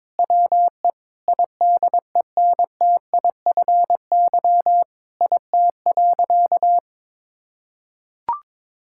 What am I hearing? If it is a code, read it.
Morse code, heard as a keyed tone, about 22 wpm, 700 Hz. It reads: WE IDENTIFY IT.
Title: Sentences from Top 1000 Words - Review Part 6 - 22wpm